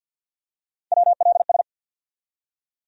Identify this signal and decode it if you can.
Morse code, heard as a keyed tone, about 52 wpm, 700 Hz. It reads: WLS
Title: USA Call Sign Postfixes - 3 Letters 50wpm